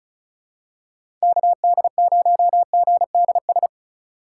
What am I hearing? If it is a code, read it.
Morse code, heard as a keyed tone, about 36 wpm, 700 Hz. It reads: KD0GDS
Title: USA Call Signs - 6 Characters - 2x Character Spacing - 35wpm